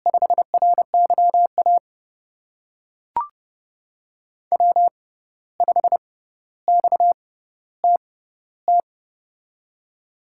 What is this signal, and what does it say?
Morse code, heard as a keyed tone, about 30 wpm, spaced as for 14 wpm, 700 Hz. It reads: 5RYA W5XTT
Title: USA Call Signs - 6x Character Spacing - 30wpm